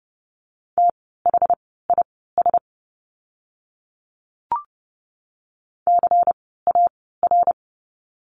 Morse code, heard as a keyed tone, about 30 wpm, spaced as for 17 wpm, 700 Hz. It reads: THIS CAR
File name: course / 30:10wpm / 108 Lesson 18f - Letter m - Words 30wpm